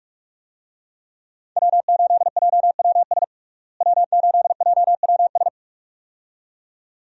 Morse code, heard as a keyed tone, about 46 wpm, 700 Hz. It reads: W8JWS W8JWS
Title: USA Call Signs - 5 Characters - Repeated 2x - 45wpm